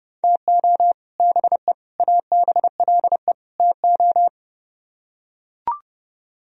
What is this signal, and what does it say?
Morse code, heard as a keyed tone, about 32 wpm, 700 Hz. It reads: TO BE ABLE TO